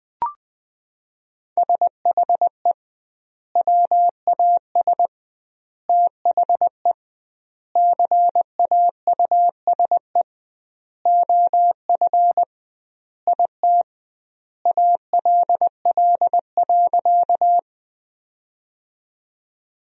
Morse code, heard as a keyed tone, about 20 wpm, 700 Hz. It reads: SHE WAS THE CAUSE OF IT ALL.
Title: 2x Word-Spacing - Sentences from Top 200 Words - 20wpm